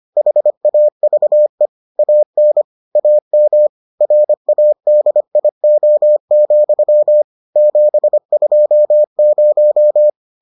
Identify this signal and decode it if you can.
Morse code, heard as a keyed tone, about 25 wpm, 600 Hz. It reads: HAVE AN AM RADIO, 720